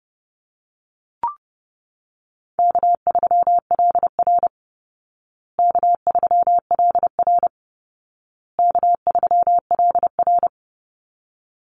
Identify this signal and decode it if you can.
Morse code, heard as a keyed tone, about 30 wpm, 700 Hz. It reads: K3LR K3LR K3LR
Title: Call Signs Repeated 2x - 30wpm